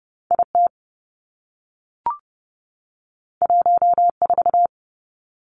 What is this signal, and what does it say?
Morse code, heard as a keyed tone, about 30 wpm, 700 Hz. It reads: I T 1 4
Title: Two Challenging Letter-Number Combinations 30wpm